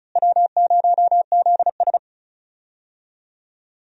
Morse code, heard as a keyed tone, about 34 wpm, 700 Hz. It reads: W0ZS
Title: USA Call Signs - 4 Characters - Repeated 3x - 35wpm